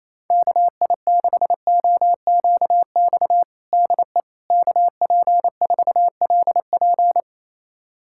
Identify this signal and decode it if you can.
Morse code, heard as a keyed tone, about 28 wpm, 700 Hz. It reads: KI6OQX DE KP4LP
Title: USA Callsign DE Callsign 28wpm